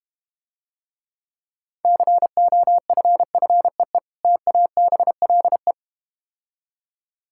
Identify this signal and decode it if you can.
Morse code, heard as a keyed tone, about 32 wpm, 700 Hz. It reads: COFFEE TABLE